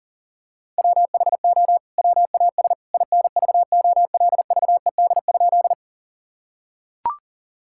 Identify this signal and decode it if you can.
Morse code, heard as a keyed tone, about 40 wpm, 700 Hz. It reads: WHO WAS INVOLVED?
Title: Sentences from Top 1000 Words - Review Part 5 - 40wpm